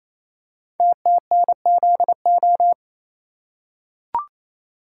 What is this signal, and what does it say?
Morse code, heard as a keyed tone, about 28 wpm, 700 Hz. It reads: TTNZO
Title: Code Groups - 5 Letters - 28wpm